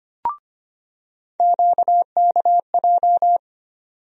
Morse code, heard as a keyed tone, about 25 wpm, 700 Hz. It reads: QKJ
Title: USA Call Sign Postfixes - 3 Letters 25wpm